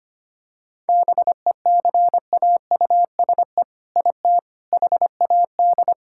Morse code, heard as a keyed tone, about 25 wpm, 700 Hz. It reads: BECAUSE IT HAD
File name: practice / 25wpm / All-Age Appropriate Jokes - 25wpm